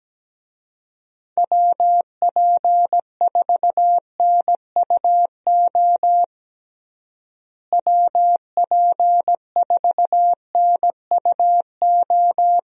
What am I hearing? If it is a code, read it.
Morse code, heard as a keyed tone, about 17 wpm, 700 Hz. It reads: WP4NUO WP4NUO
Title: USA Call Signs - 6 Characters - Repeated 3x - 17wpm